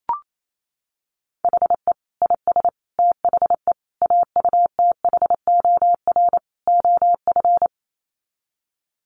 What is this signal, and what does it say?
Morse code, heard as a keyed tone, about 28 wpm, 700 Hz. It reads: HE IS THE AUTHOR OF
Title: Sets of 5 Words Encore2 28wpm